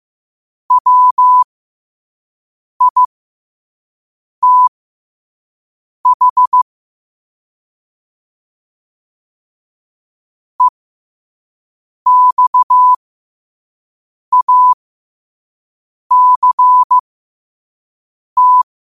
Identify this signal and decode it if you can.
Morse code, heard as a keyed tone, about 15 wpm, spaced as for 5 wpm, 1000 Hz. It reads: WITH EXACT
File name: qotdfiles / 5wpm